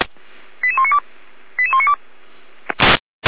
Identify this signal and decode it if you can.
Five-tone selective call (ZVEI), heard as quick runs of five tones, tones of about 70 ms. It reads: 88192 88192